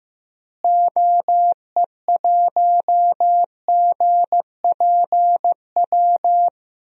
Morse code, heard as a keyed tone, about 15 wpm, 700 Hz. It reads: OE1GPW
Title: DXCC-100 Call Signs - 6 Characters - Repeated 3x - 15wpm